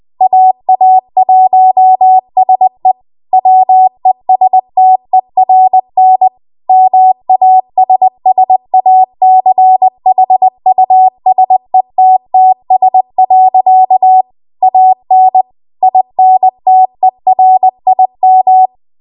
Morse code, heard as a keyed tone, about 20 wpm, 750 Hz. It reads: AA1SE WESTERN MASSACHUSETTS. AN INTERIM
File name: Archive / 20 WPM / 20WPM